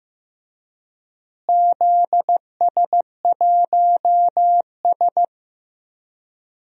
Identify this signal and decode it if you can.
Morse code, heard as a keyed tone, about 15 wpm, 700 Hz. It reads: ZS1S
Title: DXCC-100 Call Signs - 4 Characters - Repeated 2x - 15wpm